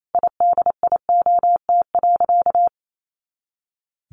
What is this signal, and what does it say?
Morse code, heard as a keyed tone, about 28 wpm, 700 Hz. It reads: IDIOT.
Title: A Sound Of Thunder 28wpm